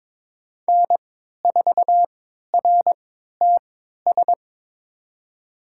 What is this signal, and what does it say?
Morse code, heard as a keyed tone, about 21 wpm, spaced as for 12 wpm, 700 Hz. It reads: N4RTS
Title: USA Call Signs - 3x Character Spacing - 22wpm